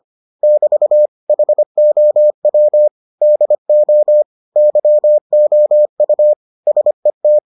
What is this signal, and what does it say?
Morse code, heard as a keyed tone, about 25 wpm, 600 Hz. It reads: <BT> HOW DO YOU SET